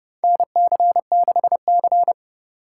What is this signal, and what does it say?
Morse code, heard as a keyed tone, about 30 wpm, 700 Hz. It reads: NC6C
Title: USA Call Signs - 4 Characters - 4x Character Spacing - 30wpm